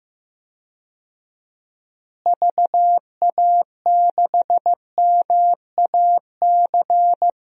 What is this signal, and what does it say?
Morse code, heard as a keyed tone, about 15 wpm, 700 Hz. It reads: VA6MAC
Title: DXCC-100 Call Signs - 6 Characters - Repeated 3x - 15wpm